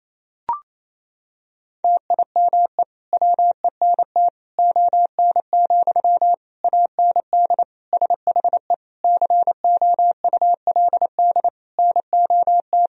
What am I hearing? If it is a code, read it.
Morse code, heard as a keyed tone, about 27 wpm, 700 Hz. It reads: TIME WENT ON, AND SHE COULD NOT
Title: Sentences from Top 1000 Words - Review Part 3 - 28wpm